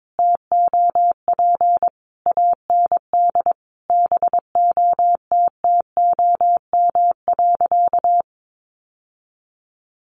Morse code, heard as a keyed tone, about 22 wpm, 700 Hz. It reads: TOP AND BOTTOM.
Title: Sentences from Top 1400 Words 22wpm